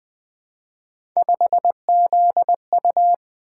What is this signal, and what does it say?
Morse code, heard as a keyed tone, about 20 wpm, 700 Hz. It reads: 5ZU
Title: Three Letter-Number Combinations - 20wpm